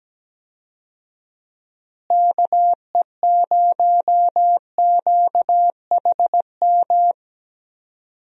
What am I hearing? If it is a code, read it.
Morse code, heard as a keyed tone, about 17 wpm, 700 Hz. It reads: KE0QHM
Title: USA Call Signs - 6 Characters - Repeated 2x - 17wpm